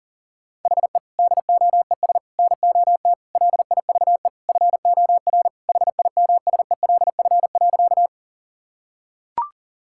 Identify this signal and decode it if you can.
Morse code, heard as a keyed tone, about 40 wpm, 700 Hz. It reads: HE DOES NOT LIVE FOR HIMSELF.
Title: Sentences from Top 1600 Words 40wpm